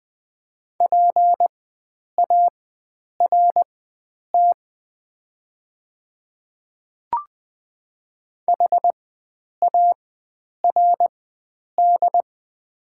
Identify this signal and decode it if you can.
Morse code, heard as a keyed tone, about 20 wpm, spaced as for 9 wpm, 700 Hz. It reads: PART HARD
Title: 4x Character Spacing - Top 300 Words - 20wpm